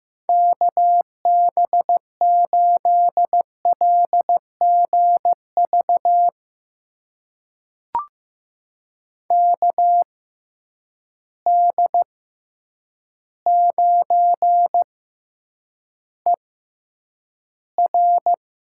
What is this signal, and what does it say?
Morse code, heard as a keyed tone, about 15 wpm, spaced as for 8 wpm, 700 Hz. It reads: KB8LGV KD9ER
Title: USA Call Signs - 6 Characters - 6x Character Spacing - 15wpm